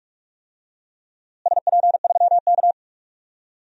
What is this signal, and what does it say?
Morse code, heard as a keyed tone, about 42 wpm, 700 Hz. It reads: SP3K
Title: DXCC-100 Call Signs - 4 Characters - Repeated 3x - 45wpm